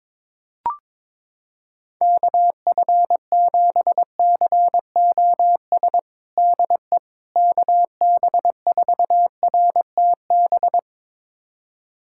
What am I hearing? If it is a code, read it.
Morse code, heard as a keyed tone, about 22 wpm, 700 Hz. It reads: KF7COS DE KB4RTB
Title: USA Callsign DE Callsign - 6 Characters - 22wpm